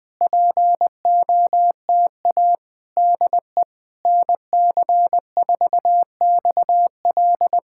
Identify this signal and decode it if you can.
Morse code, heard as a keyed tone, about 20 wpm, 690 Hz. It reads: POTA DE NC4XL
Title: POTA QSO Practice - 20wpm